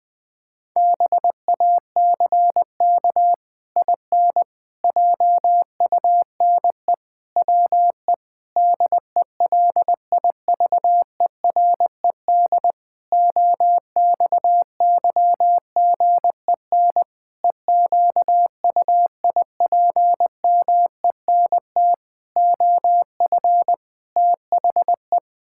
Morse code, heard as a keyed tone, about 21 wpm, 700 Hz. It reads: BACK IN JUNE WE DELIVERED OXYGEN EQUIPMENT OF THE
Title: Pangram Challenge v2 - 20wpm